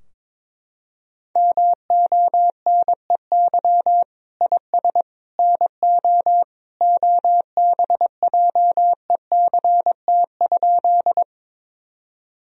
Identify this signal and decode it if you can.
Morse code, heard as a keyed tone, about 21 wpm, 700 Hz. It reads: MONEY IS NO OBJECT?
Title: Sentences from Top 1000 Words - Review Part 3 - 22wpm